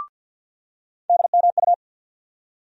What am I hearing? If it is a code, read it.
Morse code, heard as a keyed tone, about 50 wpm, 690 Hz. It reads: DMU
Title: USA Call Sign Postfixes - 3 Letters 50wpm